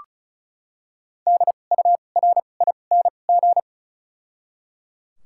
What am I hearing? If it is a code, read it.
Morse code, heard as a keyed tone, about 36 wpm, 700 Hz. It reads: D U R I N G
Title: 2x Character Spacing - Top 500 Words - 35wpm